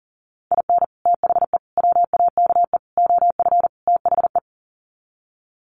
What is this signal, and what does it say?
Morse code, heard as a keyed tone, about 40 wpm, 700 Hz. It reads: IN THE WAKE OF THE
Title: Sets of 5 Words 40wpm